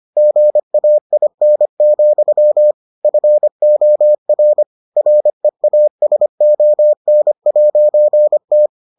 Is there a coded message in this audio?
Yes: Morse code from 0.2 to 8.7 s, heard as a keyed tone, about 25 wpm, 600 Hz. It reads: GAIN, FOR REASON'T